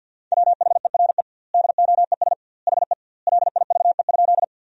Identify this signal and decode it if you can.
Morse code, heard as a keyed tone, about 52 wpm, 700 Hz. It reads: WHERE DOES HE LIVE?
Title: 2x Word-Spacing - Sentences from Top 200 Words - 50wpm